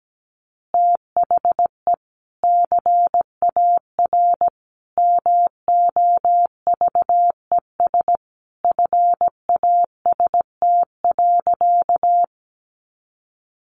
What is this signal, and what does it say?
Morse code, heard as a keyed tone, about 17 wpm, 700 Hz. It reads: THE CAR MOVES FAST.
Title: Easy Phrases 17wpm